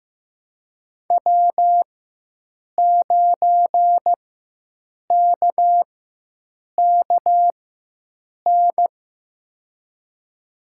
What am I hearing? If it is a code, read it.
Morse code, heard as a keyed tone, about 15 wpm, spaced as for 7 wpm, 700 Hz. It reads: W9KKN
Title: USA Call Signs - 4x Character Spacing - 15wpm